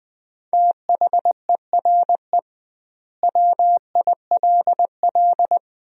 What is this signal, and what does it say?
Morse code, heard as a keyed tone, about 21 wpm, 700 Hz. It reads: THERE WILL
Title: 2x Word-Spacing - Sentences from Top 200 Words - 20wpm